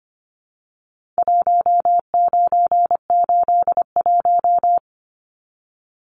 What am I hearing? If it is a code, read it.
Morse code, heard as a keyed tone, about 25 wpm, 700 Hz. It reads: 1981
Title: Numbers - Contest Style - 4-Digit 25wpm